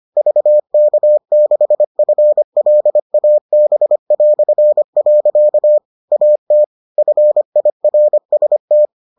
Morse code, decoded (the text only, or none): VK6FLAB". AT FIRST